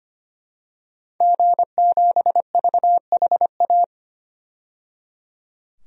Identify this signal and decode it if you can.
Morse code, heard as a keyed tone, about 25 wpm, 700 Hz. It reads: G7VHA
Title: DXCC-100 Call Signs - 5 Characters - 25wpm